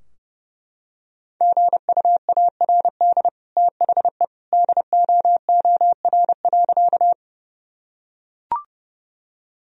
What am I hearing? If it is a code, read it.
Morse code, heard as a keyed tone, about 32 wpm, 700 Hz. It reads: GUARD THE DOOR.
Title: Sentences from Top 1000 Words - Review Part 4 - 30wpm